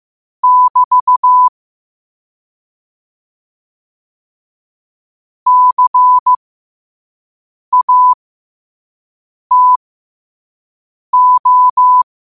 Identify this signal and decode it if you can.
Morse code, heard as a keyed tone, about 15 wpm, spaced as for 5 wpm, 1000 Hz. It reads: <BT> CATO
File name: qotdfiles / 5wpm